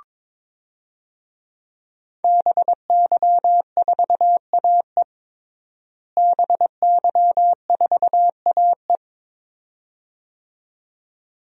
Morse code, heard as a keyed tone, about 21 wpm, 700 Hz. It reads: BY4AE BY4AE
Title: DXCC-100 Call Signs - 5 Characters - Repeated 2x - 22wpm